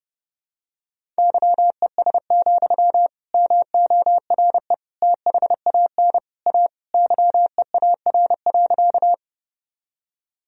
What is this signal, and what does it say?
Morse code, heard as a keyed tone, about 30 wpm, 700 Hz. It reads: YES, MORE THAN A YEAR.
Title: Sentences from Top 1000 Words - Review Part 1 - 30wpm